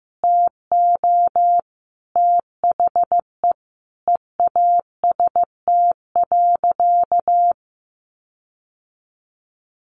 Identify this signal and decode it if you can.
Morse code, heard as a keyed tone, about 15 wpm, 700 Hz. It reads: TO THE EAST.
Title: Sentences from Top 1300 Words 15wpm